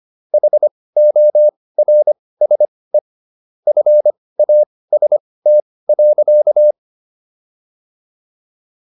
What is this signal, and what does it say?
Morse code, heard as a keyed tone, about 25 wpm, spaced as for 18 wpm, 600 Hz. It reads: HORSE FAST.